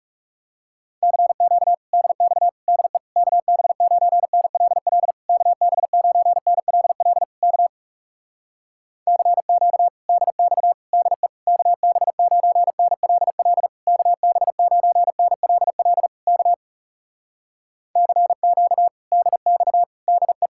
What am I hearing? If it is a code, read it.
Morse code, heard as a keyed tone, about 40 wpm, 700 Hz. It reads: CQ DX DE KB9NLL KB9NLL K CQ DX DE KB9NLL KB9NLL K CQ DX DE